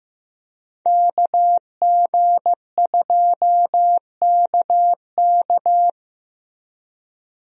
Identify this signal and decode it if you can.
Morse code, heard as a keyed tone, about 15 wpm, 700 Hz. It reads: KG2KK
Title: USA Call Signs - 5 Characters - Repeated 3x - 15wpm